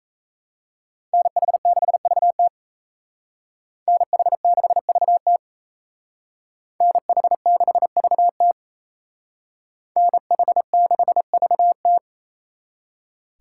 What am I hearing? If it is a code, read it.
Morse code, heard as a keyed tone, about 35 wpm, 700 Hz. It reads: NH6VT NH6VT NH6VT NH6VT